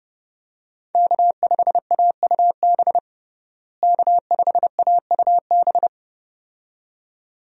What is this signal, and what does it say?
Morse code, heard as a keyed tone, about 32 wpm, 700 Hz. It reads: K5AUB K5AUB
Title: USA Call Signs - 5 Characters - Repeated 2x - 30wpm